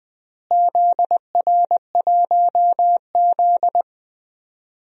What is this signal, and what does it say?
Morse code, heard as a keyed tone, about 20 wpm, 700 Hz. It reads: ZR1Z
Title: DXCC-100 Call Signs - 4 Characters - Repeated 3x - 20wpm